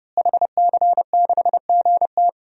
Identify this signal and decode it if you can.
Morse code, heard as a keyed tone, about 32 wpm, 700 Hz. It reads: HC6GT
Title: DXCC-100 Call Signs - 5 Characters - Repeated 3x - 30wpm